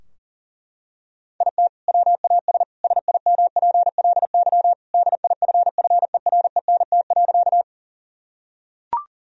Morse code, heard as a keyed tone, about 40 wpm, 700 Hz. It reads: IT WAS SIMPLY DIFFERENT.
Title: Sentences from Top 1000 Words - Review Part 4 - 40wpm